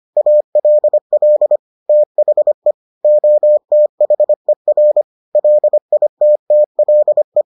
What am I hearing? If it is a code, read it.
Morse code, heard as a keyed tone, about 25 wpm, 600 Hz. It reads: ALL THE OTHER LITTLE